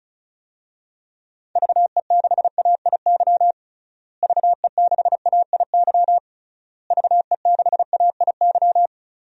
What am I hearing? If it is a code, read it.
Morse code, heard as a keyed tone, about 36 wpm, 700 Hz. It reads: VE6AIY VE6AIY VE6AIY
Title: DXCC-100 Call Signs - 6 Characters - Repeated 3x - 35wpm